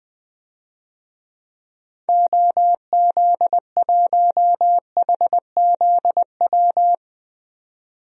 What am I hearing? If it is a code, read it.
Morse code, heard as a keyed tone, about 21 wpm, 700 Hz. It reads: OZ1HZW
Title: DXCC-100 Call Signs - 6 Characters - 20wpm